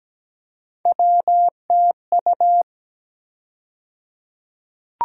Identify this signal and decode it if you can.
Morse code, heard as a keyed tone, about 17 wpm, 700 Hz. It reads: WTU
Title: USA Call Sign Postfixes - 3 Letters 17wpm